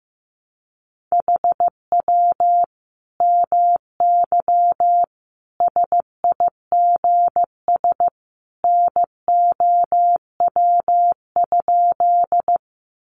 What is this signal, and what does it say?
Morse code, heard as a keyed tone, about 15 wpm, 700 Hz. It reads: HW MY SIGS NOW?
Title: Common QSO Phrases 15wpm